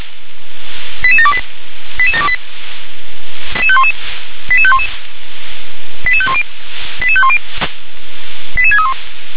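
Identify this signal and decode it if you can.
Five-tone selective call (ZVEI), heard as quick runs of five tones, tones of about 70 ms. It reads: 88728 88728 88511 88510 88510 88631
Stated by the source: ZVEI five-tone selective call 88731, received by SDR.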